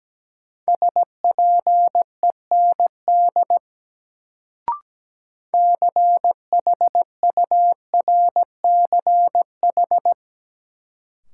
Morse code, heard as a keyed tone, about 17 wpm, 700 Hz. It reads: SPEND CHURCH
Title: Sets of 5 Words - New Word Drill - 17wpm